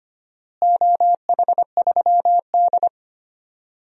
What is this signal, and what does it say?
Morse code, heard as a keyed tone, about 25 wpm, 700 Hz. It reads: OH3D
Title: DXCC-100 Call Signs - 4 Characters - Repeated 3x - 25wpm